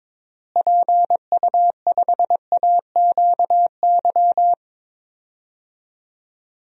Morse code, heard as a keyed tone, about 22 wpm, 700 Hz. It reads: PU5AQY
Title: DXCC-100 Call Signs - 6 Characters - Repeated 2x - 22wpm